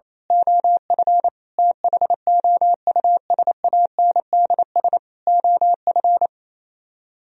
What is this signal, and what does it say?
Morse code, heard as a keyed tone, about 26 wpm, 700 Hz. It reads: OF THOUSANDS OF
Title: Speed-Racing - Sets of 4 Words 28wpm